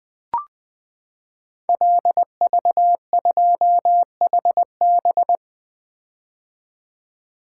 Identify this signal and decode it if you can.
Morse code, heard as a keyed tone, about 20 wpm, 700 Hz. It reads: LV2HB
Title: Code Groups - 5 Letter:Numbers - 20wpm